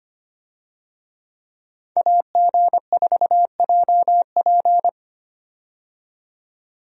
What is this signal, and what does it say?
Morse code, heard as a keyed tone, about 25 wpm, 690 Hz. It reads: AG4JP
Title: USA Call Signs - 5 Characters - 25wpm